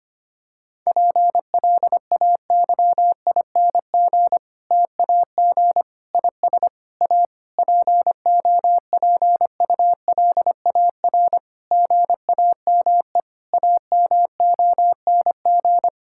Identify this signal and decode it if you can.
Morse code, heard as a keyed tone, about 25 wpm, 700 Hz. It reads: PLAYING TAG IS A POPULAR GAME AMONG
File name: practice / 25wpm / Difficult Sentences with Hints - 25wpm